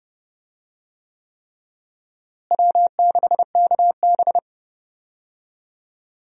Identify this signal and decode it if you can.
Morse code, heard as a keyed tone, about 30 wpm, 700 Hz. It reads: W6KB
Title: USA Call Signs - 4 Characters - Repeated 3x - 30wpm